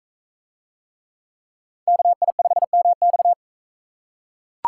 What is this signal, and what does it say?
Morse code, heard as a keyed tone, about 40 wpm, 700 Hz. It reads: KI5MX